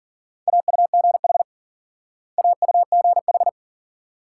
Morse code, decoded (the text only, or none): AUGH AUGH